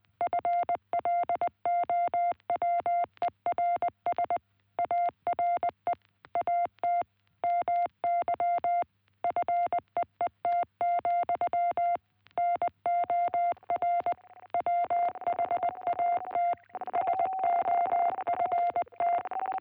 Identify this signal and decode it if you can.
Morse code, heard as a keyed tone, about 18 wpm, 660 Hz. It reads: FLOWERS ARE AT MY FEET, NOR WHAT SOFT